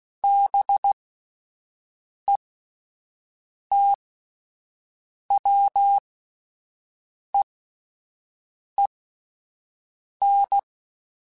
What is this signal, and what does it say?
Morse code, heard as a keyed tone, about 16 wpm, spaced as for 5 wpm, 800 Hz. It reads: BETWEEN
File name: Spaced Characters / Spaced Characters-mp3s / Weather4 21 sp @10